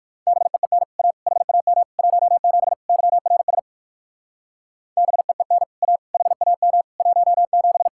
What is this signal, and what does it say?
Morse code, heard as a keyed tone, about 51 wpm, 700 Hz. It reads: BEEN A HAM 17 YRS BEEN A HAM 17
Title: Speed-Racing Common QSO Phrases Encore - 45wpm